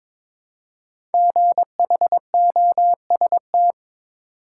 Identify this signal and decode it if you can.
Morse code, heard as a keyed tone, about 22 wpm, 700 Hz. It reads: GHOST
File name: 5 Letter Words - First 3 Letters Given - 22wpm